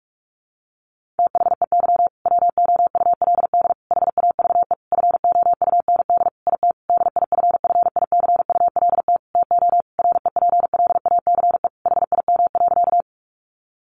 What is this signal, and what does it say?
Morse code, heard as a keyed tone, about 44 wpm, 700 Hz. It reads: THEY WOULD HAVE FOUND IT DIFFICULT TO REPLACE HIM.
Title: Sentences from Top 1400 Words 45wpm